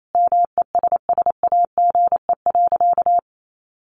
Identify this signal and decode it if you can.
Morse code, heard as a keyed tone, about 28 wpm, 700 Hz. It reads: MESSAGE.
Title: Sentences from Top 1100 Words 28wpm